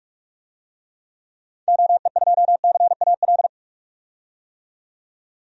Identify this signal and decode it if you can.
Morse code, heard as a keyed tone, about 47 wpm, 700 Hz. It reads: OE2CAL
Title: DXCC-100 Call Signs - 6 Characters - 45wpm